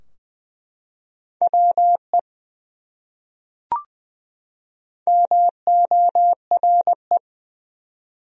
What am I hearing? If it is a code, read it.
Morse code, heard as a keyed tone, about 20 wpm, 700 Hz. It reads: WE MORE
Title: Top 100 Words 20wpm